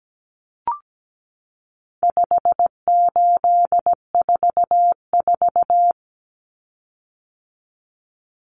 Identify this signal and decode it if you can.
Morse code, heard as a keyed tone, about 17 wpm, 700 Hz. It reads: 5844
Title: Common Cut Numbers - 4-Digit 17wpm